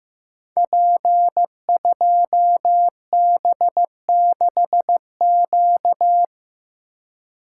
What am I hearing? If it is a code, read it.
Morse code, heard as a keyed tone, about 15 wpm, 700 Hz. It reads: P2B6Q
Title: Code Groups - 5 Letter:Numbers - 15wpm